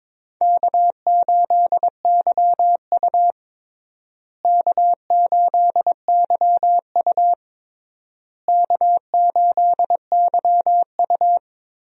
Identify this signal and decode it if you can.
Morse code, heard as a keyed tone, about 22 wpm, 700 Hz. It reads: K8YU K8YU K8YU
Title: USA Call Signs - 4 Characters - Repeated 3x - 22wpm